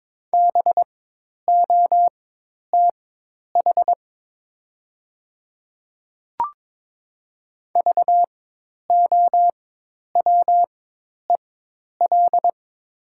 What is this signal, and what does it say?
Morse code, heard as a keyed tone, about 22 wpm, spaced as for 10 wpm, 700 Hz. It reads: BOTH VOWEL
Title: 4x Character Spacing - Top 500 Words - 22wpm